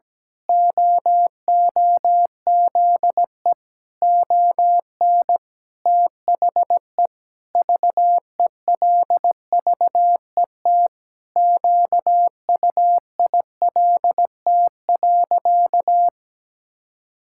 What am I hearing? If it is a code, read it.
Morse code, heard as a keyed tone, about 17 wpm, 700 Hz. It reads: OOZE ON THE VELVET QUILT.
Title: Pangram Challenge v2 - 17wpm